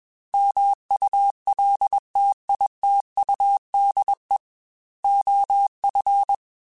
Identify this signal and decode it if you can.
Morse code, heard as a keyed tone, about 21 wpm, 800 Hz. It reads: MULTITUDE OF
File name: Technical Subjects / Technical Subjects-mp3s / OR Gate @20